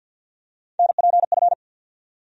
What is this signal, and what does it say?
Morse code, heard as a keyed tone, about 53 wpm, 700 Hz. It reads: NPF
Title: USA Call Sign Postfixes - 3 Letters 50wpm